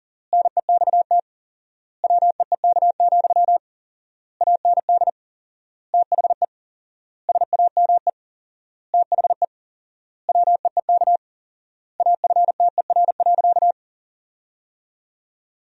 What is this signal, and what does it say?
Morse code, heard as a keyed tone, about 40 wpm, 700 Hz. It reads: NEXT WEEK, AND THE SAME THE WEEK AFTER.
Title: 4x Word-Spacing - Sentences from Top 500 Words - 40wpm